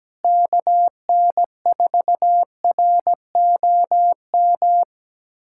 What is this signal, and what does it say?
Morse code, heard as a keyed tone, about 17 wpm, 700 Hz. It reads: KN4ROM
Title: USA Callsign DE Callsign 17wpm